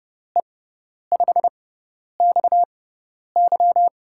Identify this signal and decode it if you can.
Morse code, heard as a keyed tone, about 32 wpm, spaced as for 10 wpm, 700 Hz. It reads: E5XY
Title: USA Call Signs - 5 Characters - 6x Character Spacing - 30wpm